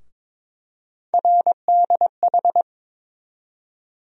Morse code, heard as a keyed tone, about 22 wpm, 700 Hz. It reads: RDH
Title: USA Call Sign Postfixes - 3 Letters 22wpm